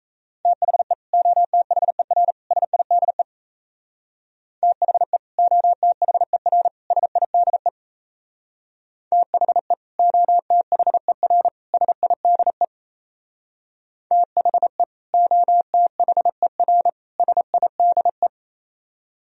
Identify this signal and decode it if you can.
Morse code, heard as a keyed tone, about 35 wpm, 700 Hz. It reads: THE OTHER SIDE THE OTHER SIDE THE OTHER SIDE THE OTHER SIDE